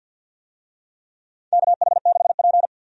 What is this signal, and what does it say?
Morse code, heard as a keyed tone, about 51 wpm, 700 Hz. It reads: KH6P
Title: DXCC-100 Call Signs - 4 Characters - 50wpm